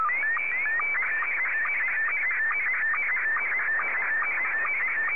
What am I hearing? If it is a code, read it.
SSTV (slow-scan television), Scottie 1.
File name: sstv / S1-VK2EX